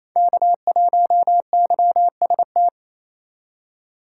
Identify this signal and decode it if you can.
Morse code, heard as a keyed tone, about 35 wpm, 700 Hz. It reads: K1YST